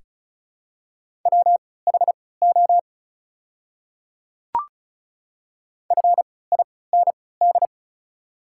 Morse code, heard as a keyed tone, about 36 wpm, spaced as for 20 wpm, 700 Hz. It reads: WHO FIND
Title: 3x Character Spacing - Top 100 Words - 35wpm